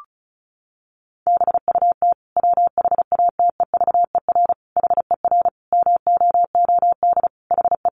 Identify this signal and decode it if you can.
Morse code, heard as a keyed tone, about 34 wpm, 700 Hz. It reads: BUT WHATEVER HER MOOD HE
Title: Sentences from Top 1100 Words 35wpm